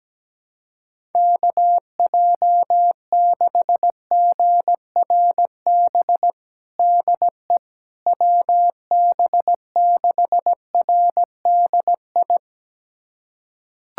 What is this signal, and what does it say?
Morse code, heard as a keyed tone, about 17 wpm, 700 Hz. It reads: KJ6GRB DE WB6RDI
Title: USA Callsign DE Callsign - 6 Characters - 17wpm